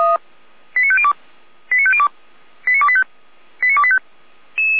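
Five-tone selective call (ZVEI), heard as quick runs of five tones, tones of about 70 ms. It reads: 89502 89502 89285 89285
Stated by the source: ZVEI five-tone selective call received on SDR.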